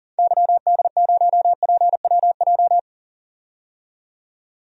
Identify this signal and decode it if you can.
Morse code, heard as a keyed tone, about 40 wpm, 700 Hz. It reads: YD0PWJ
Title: DXCC-100 Call Signs - 6 Characters - 40wpm